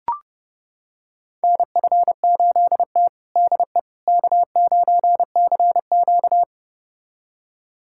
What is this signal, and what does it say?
Morse code, heard as a keyed tone, about 30 wpm, 700 Hz. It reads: NF8T DE K9CQ
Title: USA Callsign DE Callsign - 4 Characters - 30wpm